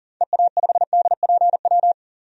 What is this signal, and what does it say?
Morse code, heard as a keyed tone, about 40 wpm, 700 Hz. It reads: EA5DPW
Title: DXCC-100 Call Signs - 6 Characters - 40wpm